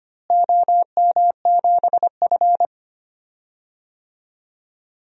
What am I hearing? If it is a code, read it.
Morse code, heard as a keyed tone, about 25 wpm, 700 Hz. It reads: OM7F
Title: DXCC-100 Call Signs - 4 Characters - Repeated 2x - 25wpm